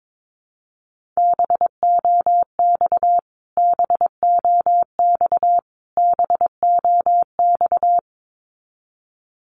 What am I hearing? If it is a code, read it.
Morse code, heard as a keyed tone, about 22 wpm, 700 Hz. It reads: BOX BOX BOX
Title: Top 500 Words Repeated 22wpm